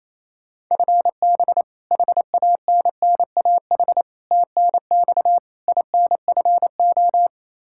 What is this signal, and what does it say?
Morse code, heard as a keyed tone, about 28 wpm, 700 Hz. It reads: FB HANNAH TNX INFO
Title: Practice QSO - Exchange 3 - More Info - 28wpm